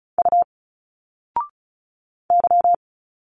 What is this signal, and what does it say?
Morse code, heard as a keyed tone, about 36 wpm, 700 Hz. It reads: U Y
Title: Single Letters 35wpm